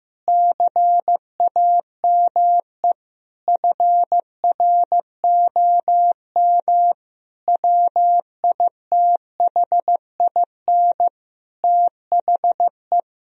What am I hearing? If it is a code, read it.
Morse code, heard as a keyed tone, about 16 wpm, 700 Hz. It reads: CAME FROM WITHIN THE
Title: Sentences from Top 1000 Words - Review Part 4 - 15wpm